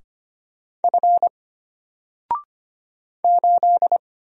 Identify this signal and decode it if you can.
Morse code, heard as a keyed tone, about 25 wpm, 700 Hz. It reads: F 8
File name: Single-Letter-Number-25wpm